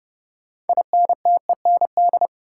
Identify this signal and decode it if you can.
Morse code, heard as a keyed tone, about 32 wpm, 700 Hz. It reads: INTEND
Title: Sentences from Top 1600 Words - New Word Drill Alt - 30wpm